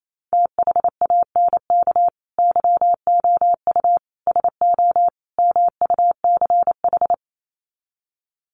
Morse code, heard as a keyed tone, about 28 wpm, 700 Hz. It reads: THANK YOU SO MUCH
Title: Sets of 4 Words 28wpm